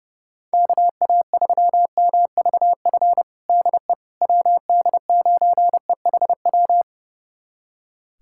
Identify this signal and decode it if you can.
Morse code, heard as a keyed tone, about 30 wpm, 690 Hz. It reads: KA3MVF DE WD9EHW
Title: USA Callsign DE Callsign 30wpm